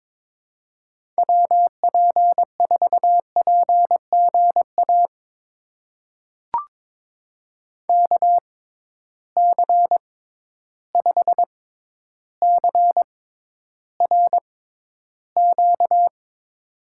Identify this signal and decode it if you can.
Morse code, heard as a keyed tone, about 22 wpm, spaced as for 11 wpm, 700 Hz. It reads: WP4PGA KC5CRQ
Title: USA Call Signs - 6 Characters - 6x Character Spacing - 22wpm